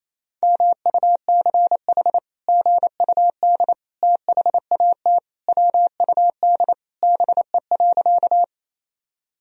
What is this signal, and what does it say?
Morse code, heard as a keyed tone, about 28 wpm, 700 Hz. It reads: MUCH GUD THAT WUD BE.
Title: Sentences from Top 200 Words and Abbreviations 28wpm